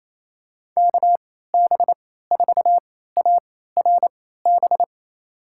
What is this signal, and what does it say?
Morse code, heard as a keyed tone, about 28 wpm, spaced as for 16 wpm, 700 Hz. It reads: KB4ARB
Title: USA Call Signs - 6 Characters - 3x Character Spacing - 28wpm